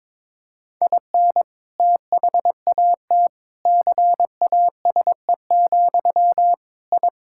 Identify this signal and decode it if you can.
Morse code, heard as a keyed tone, about 22 wpm, 700 Hz. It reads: IN THAT CASE, I
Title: Sentences from Top 1000 Words - Review Part 6 - 22wpm